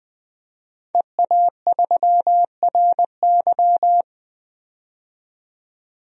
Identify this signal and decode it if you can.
Morse code, heard as a keyed tone, about 20 wpm, 700 Hz. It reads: EA3RY
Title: DXCC-100 Call Signs - 5 Characters - 20wpm